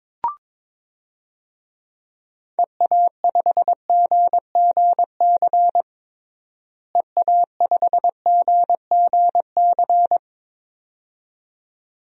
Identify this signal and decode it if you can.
Morse code, heard as a keyed tone, about 22 wpm, 700 Hz. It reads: EA5GGC EA5GGC
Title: DXCC-100 Call Signs - 6 Characters - Repeated 2x - 22wpm